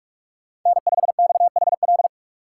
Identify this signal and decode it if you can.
Morse code, heard as a keyed tone, about 42 wpm, 700 Hz. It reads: N5XHL
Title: USA Call Signs - 5 Characters - 3x Character Spacing - 45wpm